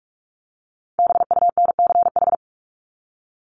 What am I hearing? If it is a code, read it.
Morse code, heard as a keyed tone, about 44 wpm, 700 Hz. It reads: BUNCH
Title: Top 1751-2000 Words 45wpm